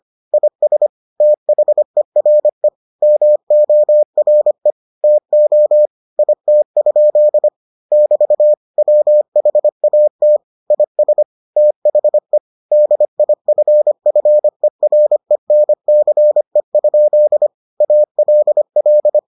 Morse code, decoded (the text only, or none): IS THERE MORE TO IT? <BT> WHAT IS THE DIFFERENCE? ALL